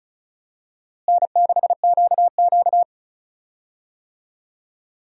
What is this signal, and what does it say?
Morse code, heard as a keyed tone, about 34 wpm, 700 Hz. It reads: N6QQ
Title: USA Call Signs - 4 Characters - Repeated 2x - 35wpm